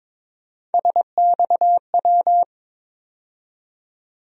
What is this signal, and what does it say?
Morse code, heard as a keyed tone, about 22 wpm, 700 Hz. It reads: SXW
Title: USA Call Sign Postfixes - 3 Letters 22wpm